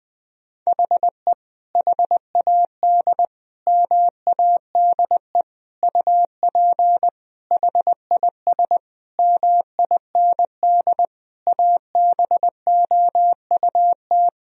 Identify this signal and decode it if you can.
Morse code, heard as a keyed tone, about 20 wpm, 700 Hz. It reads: HE HAD MADE UP HIS MIND ABOUT
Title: Sentences from Top 1000 Words - Review Part 3 - 20wpm